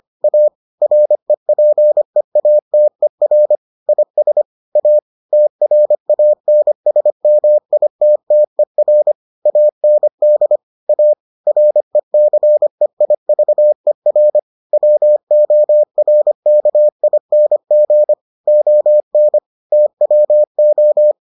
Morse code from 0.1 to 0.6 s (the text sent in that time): A